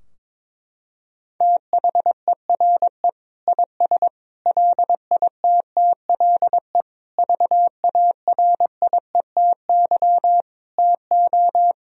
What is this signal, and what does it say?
Morse code, heard as a keyed tone, about 21 wpm, 700 Hz. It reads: THERE IS LITTLE VARIETY TO
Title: Sentences from Top 1000 Words - Review Part 4 - 22wpm